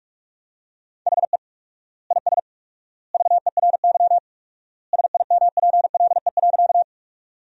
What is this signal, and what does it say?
Morse code, heard as a keyed tone, about 46 wpm, 700 Hz. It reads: HE IS VERY SIMPLE.
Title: 4x Word-Spacing - Sentences from Top 400 Words - 45wpm